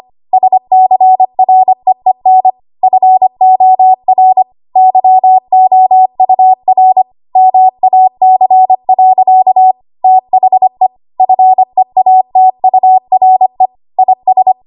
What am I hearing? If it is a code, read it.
Morse code, heard as a keyed tone, about 25 wpm, 750 Hz. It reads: SCREEN FOR YOUR MAC. THE FEATURE IS
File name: Archive / 25 WPM / 25WPM